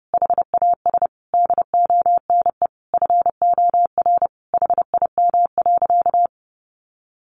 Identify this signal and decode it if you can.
Morse code, heard as a keyed tone, about 30 wpm, 700 Hz. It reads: HAS DONE FOR HIM.
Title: Sentences from Top 1400 Words 30wpm